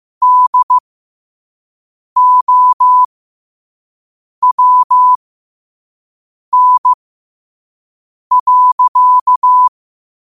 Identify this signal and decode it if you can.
Morse code, heard as a keyed tone, about 15 wpm, spaced as for 5 wpm, 1000 Hz. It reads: DOWN.
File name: qotdfiles / 5wpm